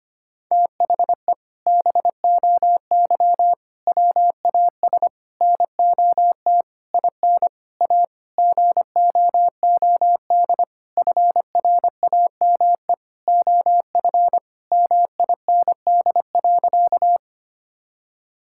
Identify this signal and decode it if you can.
Morse code, heard as a keyed tone, about 25 wpm, 700 Hz. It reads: THE BOY WAS NOT IN A GOOD FRAME OF MIND.
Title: Sentences from Top 1000 Words - Review Part 6 - 25wpm